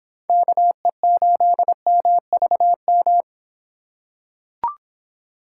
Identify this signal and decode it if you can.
Morse code, heard as a keyed tone, about 26 wpm, 700 Hz. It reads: KE8MVM